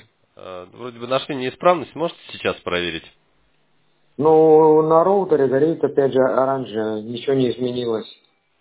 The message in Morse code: none

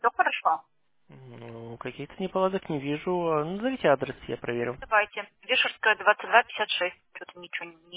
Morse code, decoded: none

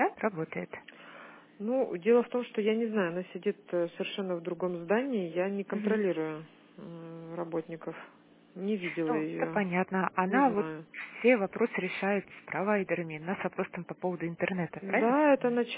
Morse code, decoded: none

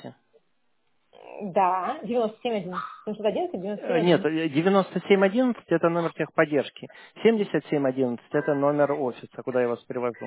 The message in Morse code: none

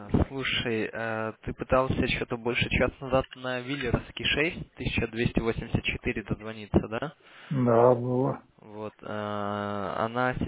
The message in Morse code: none

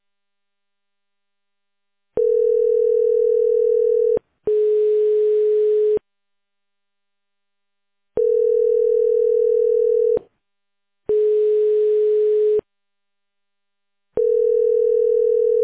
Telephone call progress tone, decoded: ringback tone

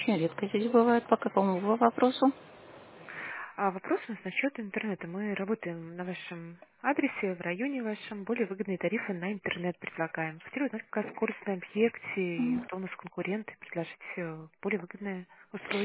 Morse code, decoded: none

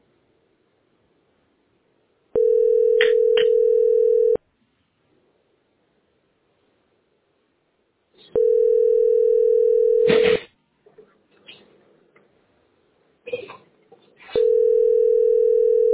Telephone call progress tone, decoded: ringback tone

